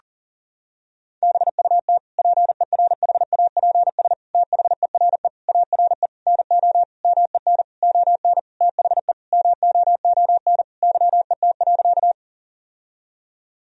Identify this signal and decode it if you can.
Morse code, heard as a keyed tone, about 40 wpm, 700 Hz. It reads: BUT PERHAPS THERE ARE NO MEN ON THE MOON YET.
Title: Sentences from Top 1000 Words - Review Part 3 - 40wpm